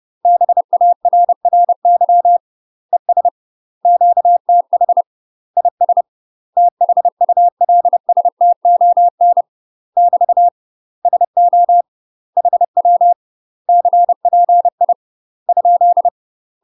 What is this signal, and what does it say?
Morse code, heard as a keyed tone, about 30 wpm, 700 Hz. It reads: BARRY ES QTH IS THULSTON <BT> SO HW CPI ?
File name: UK MASTER QSOs TRAINING / 30WPM / UK MASTER QSOs 0032